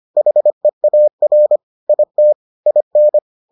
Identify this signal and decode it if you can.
Morse code, heard as a keyed tone, about 25 wpm, 600 Hz. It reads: HEAR IT IN